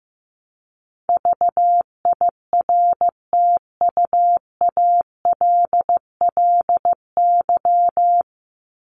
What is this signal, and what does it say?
Morse code, heard as a keyed tone, about 15 wpm, 700 Hz. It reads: VIRTUALLY
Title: Top 1501-1750 Words 15wpm